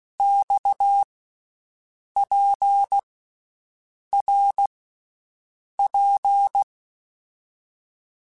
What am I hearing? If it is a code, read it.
Morse code, heard as a keyed tone, about 16 wpm, 800 Hz. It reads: X P R P
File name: Basic Letter Groups By7Groups / RKPX / RKPX@13